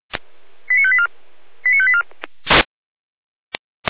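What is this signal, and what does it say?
Five-tone selective call (ZVEI), heard as quick runs of five tones, tones of about 70 ms. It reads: 89584 89584
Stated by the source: ZVEI five-tone call 89172 received by SDR.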